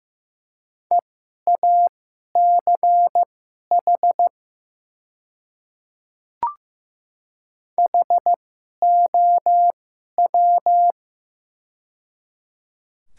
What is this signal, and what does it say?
Morse code, heard as a keyed tone, about 15 wpm, spaced as for 11 wpm, 700 Hz. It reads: EACH HOW
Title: 2x Character Spacing - Top 200 Words - 15wpm